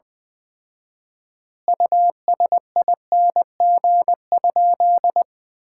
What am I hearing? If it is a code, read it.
Morse code, heard as a keyed tone, about 21 wpm, 700 Hz. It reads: USING?
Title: Common QSO Phrases - 4x Word-Spacing - 20wpm